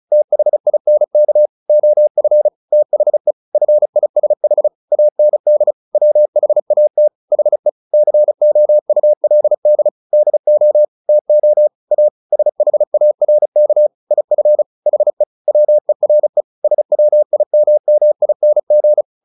Morse code, heard as a keyed tone, about 34 wpm, 600 Hz. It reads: THINK OF THE FISH AND WHAT HE COULD DO TO A SHARK IF HE WERE SWIMMING
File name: the old man and the sea 2 35wpm